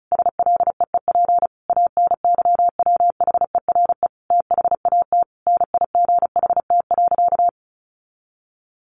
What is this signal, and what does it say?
Morse code, heard as a keyed tone, about 34 wpm, 700 Hz. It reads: SLEEP ANYWHERE THAT NIGHT.
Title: Sentences from Top 1400 Words 35wpm